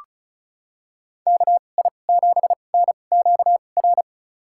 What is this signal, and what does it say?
Morse code, heard as a keyed tone, about 34 wpm, spaced as for 25 wpm, 700 Hz. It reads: KI7NQR
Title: USA Call Signs - 6 Characters - 2x Character Spacing - 35wpm